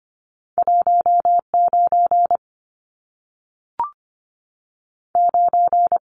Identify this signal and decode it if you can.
Morse code, heard as a keyed tone, about 25 wpm, 700 Hz. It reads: 19 9
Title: Numbers - 2-Digit 25wpm